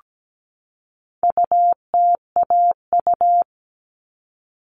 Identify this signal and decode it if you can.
Morse code, heard as a keyed tone, about 17 wpm, 700 Hz. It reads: UTAU
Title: All Cut Numbers - 4-Digit 17wpm